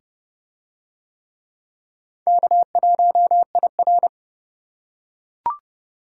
Morse code, heard as a keyed tone, about 27 wpm, 700 Hz. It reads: K1IR